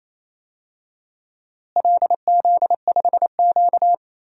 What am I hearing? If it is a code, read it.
Morse code, heard as a keyed tone, about 28 wpm, 700 Hz. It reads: LZ5Q
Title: DXCC-100 Call Signs - 4 Characters - Repeated 2x - 28wpm